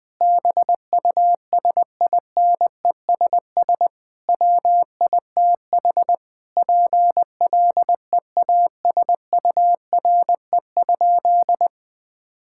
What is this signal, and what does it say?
Morse code, heard as a keyed tone, about 20 wpm, 700 Hz. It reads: BUSINESS WITH PLEASURE?
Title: Sentences from Top 1600 Words 20wpm